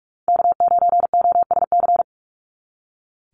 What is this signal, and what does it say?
Morse code, heard as a keyed tone, about 46 wpm, 690 Hz. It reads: K9OSC
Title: NA SOTA Chaser Callsigns 45wpm